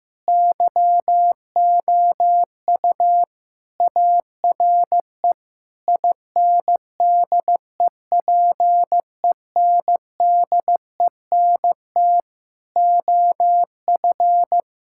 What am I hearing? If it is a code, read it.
Morse code, heard as a keyed tone, about 15 wpm, 700 Hz. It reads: YOU ARE INDEPENDENT OF
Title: Sentences from Top 1000 Words - Review Part 6 - 15wpm